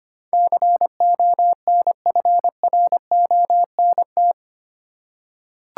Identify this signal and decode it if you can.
Morse code, heard as a keyed tone, about 25 wpm, 690 Hz. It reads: CONFRONT
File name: 8 Letter Words - First 3 Letters Given - 25wpm